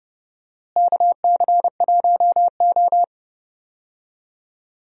Morse code, heard as a keyed tone, about 30 wpm, 700 Hz. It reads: KC1O
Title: USA Call Signs - 4 Characters - Repeated 2x - 30wpm